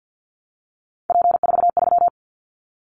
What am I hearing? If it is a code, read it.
Morse code, heard as a keyed tone, about 50 wpm, 700 Hz. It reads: P43
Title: Callsign Prefixes for Top 100 DXCC - 50wpm